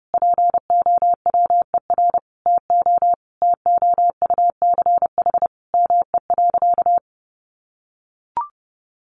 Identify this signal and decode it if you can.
Morse code, heard as a keyed tone, about 30 wpm, 700 Hz. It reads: POWER TO TOUCH ME.
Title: Sentences from Top 1200 Words 30wpm